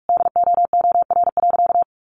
Morse code, heard as a keyed tone, about 44 wpm, 700 Hz. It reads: DOOR.
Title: Sentences from Top 300 Words 45wpm